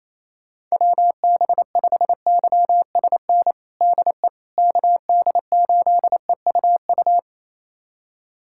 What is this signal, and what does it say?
Morse code, heard as a keyed tone, about 28 wpm, 700 Hz. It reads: WB5YSN DE KD8EUU
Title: USA Callsign DE Callsign - 6 Characters - 28wpm